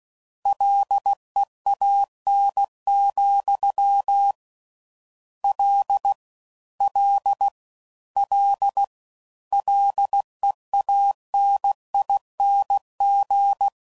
Morse code, heard as a keyed tone, about 16 wpm, 800 Hz. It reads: LEAN, L L L LEANING